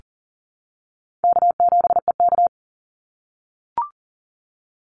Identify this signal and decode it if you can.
Morse code, heard as a keyed tone, about 40 wpm, 700 Hz. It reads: K7EK
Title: Easy Call Signs - 40wpm